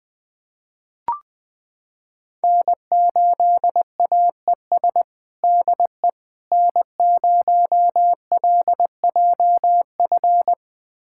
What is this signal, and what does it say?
Morse code, heard as a keyed tone, about 21 wpm, 700 Hz. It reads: N8AES DE N0LJF
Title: USA Callsign DE Callsign - 5 Characters - 20wpm